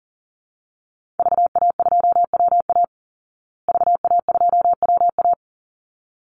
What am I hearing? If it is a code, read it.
Morse code, heard as a keyed tone, about 40 wpm, 700 Hz. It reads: VA2WA VA2WA
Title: Call Signs Repeated 3x - 40wpm